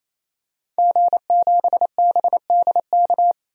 Morse code, heard as a keyed tone, about 28 wpm, 700 Hz. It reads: G7BDK
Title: DXCC-100 Call Signs - 5 Characters - 28wpm